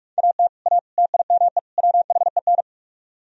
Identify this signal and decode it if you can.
Morse code, heard as a keyed tone, about 59 wpm, spaced as for 52 wpm, 700 Hz. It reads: AT A TIME WHEN
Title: Speed-Racing - Sets of 4 Words 45wpm